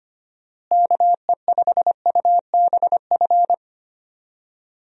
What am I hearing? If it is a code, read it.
Morse code, heard as a keyed tone, about 25 wpm, 700 Hz. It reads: KE5UBF
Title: USA Call Signs - 6 Characters - Repeated 2x - 25wpm